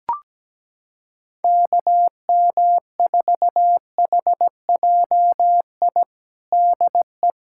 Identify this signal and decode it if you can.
Morse code, heard as a keyed tone, about 17 wpm, 700 Hz. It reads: KM4HJI DE
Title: USA Callsign DE Callsign - 6 Characters - 17wpm